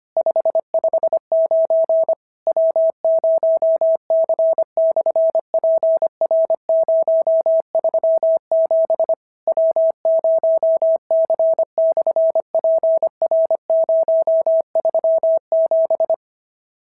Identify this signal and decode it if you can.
Morse code, heard as a keyed tone, about 25 wpm, 650 Hz. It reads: 559 W0C/PR037 W0C/PR037
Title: SOTA QSO Practice 25wpm